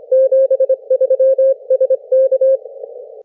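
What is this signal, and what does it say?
Morse code, heard as a keyed tone, about 25 wpm, 530 Hz. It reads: 73 SK
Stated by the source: CW QSO between UA1CEG and RX6MY on homebrew rigs, three watts QRP into a delta antenna in forest tree branches, heard thousands of kilometres away; RU-QRP Club.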